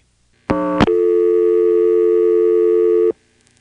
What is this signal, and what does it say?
Dial tone (North America), 350 + 440 Hz.